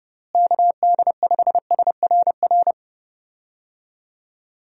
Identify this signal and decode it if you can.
Morse code, heard as a keyed tone, about 31 wpm, 700 Hz. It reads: KD5SRR
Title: USA Call Signs - 6 Characters - Repeated 3x - 30wpm